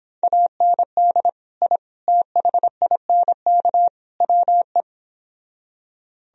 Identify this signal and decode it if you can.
Morse code, heard as a keyed tone, about 31 wpm, 700 Hz. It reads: AND I THINK WE